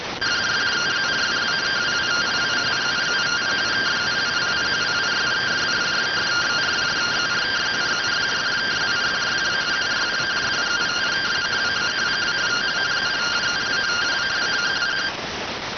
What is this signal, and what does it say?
Two-tone teleprinter signal, about 170 Hz shift.